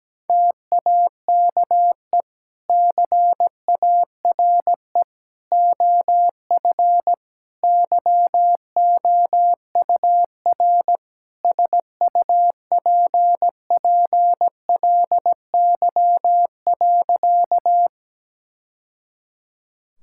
Morse code, heard as a keyed tone, about 17 wpm, 700 Hz. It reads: TAKE CARE OF YOUR SUPPLY.
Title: Sentences from Top 1000 Words - Review Part 5 - 17wpm